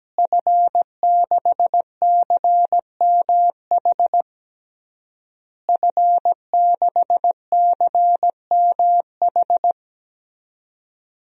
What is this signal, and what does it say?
Morse code, heard as a keyed tone, about 17 wpm, 700 Hz. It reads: F6CMH F6CMH
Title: DXCC-100 Call Signs - 5 Characters - Repeated 3x - 17wpm